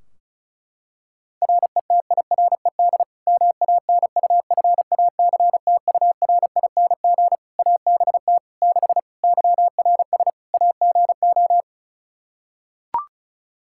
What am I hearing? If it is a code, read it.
Morse code, heard as a keyed tone, about 34 wpm, 700 Hz. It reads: RETIRED MANUFACTURING ABT 6 YRS AGO
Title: Practice QSO - Exchange 3 - More Info - 35wpm